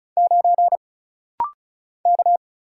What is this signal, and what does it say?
Morse code, heard as a keyed tone, about 36 wpm, 700 Hz. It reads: 9 K
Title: Single Letter-Number - Rapid-Fire 35wpm